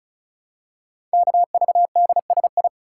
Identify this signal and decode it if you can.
Morse code, heard as a keyed tone, about 36 wpm, 700 Hz. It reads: KVDSI
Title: Code Groups - 5 Letters - 35wpm